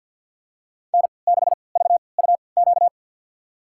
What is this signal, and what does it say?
Morse code, heard as a keyed tone, about 47 wpm, spaced as for 28 wpm, 700 Hz. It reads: N6VUQ
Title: USA Call Signs - 5 Characters - 3x Character Spacing - 50wpm